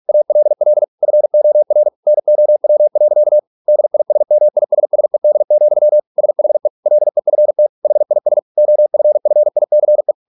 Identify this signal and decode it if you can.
Morse code, heard as a keyed tone, about 46 wpm, 600 Hz. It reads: ALL FOR NOW. DISMISSED, SHE LEFT HIS OFFICE